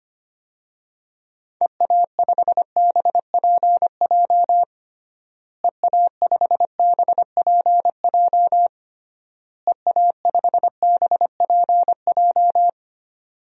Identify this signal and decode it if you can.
Morse code, heard as a keyed tone, about 25 wpm, 700 Hz. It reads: EA5BPJ EA5BPJ EA5BPJ
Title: DXCC-100 Call Signs - 6 Characters - Repeated 3x - 25wpm